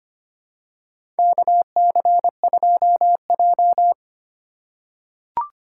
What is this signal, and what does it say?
Morse code, heard as a keyed tone, about 25 wpm, 700 Hz. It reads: KC2J
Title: USA Call Signs - 4 Characters - 4x Character Spacing - 25wpm